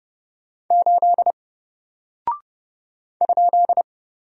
Morse code, heard as a keyed tone, about 30 wpm, 700 Hz. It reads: 8 ?
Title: Single Character (Letter:Number:Punctuation:Prosign) 30wpm